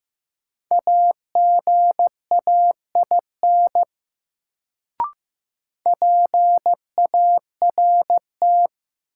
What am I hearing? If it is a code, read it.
Morse code, heard as a keyed tone, about 16 wpm, 700 Hz. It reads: AGAIN PART
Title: Sentences from Top 200 Words - New Word Drill - 15wpm